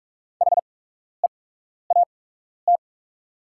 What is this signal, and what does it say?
Morse code, heard as a keyed tone, about 44 wpm, spaced as for 12 wpm, 700 Hz. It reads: HEAT
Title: 8x Character Spacing - Top 500 Words - 45wpm